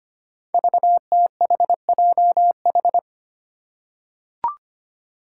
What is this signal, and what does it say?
Morse code, heard as a keyed tone, about 25 wpm, 700 Hz. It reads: VTHJH
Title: Five Challenging Letter-Number Combinations 25wpm